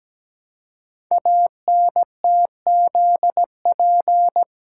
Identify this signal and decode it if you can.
Morse code, heard as a keyed tone, about 17 wpm, 700 Hz. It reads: ANTZP
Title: Code Groups - 5 Letters - 17wpm